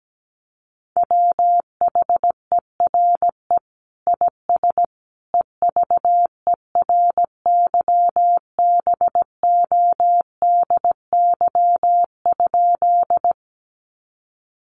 Morse code, heard as a keyed tone, about 17 wpm, 700 Hz. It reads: WHERE IS EVERYBODY?
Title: Sentences from Top 1400 Words 17wpm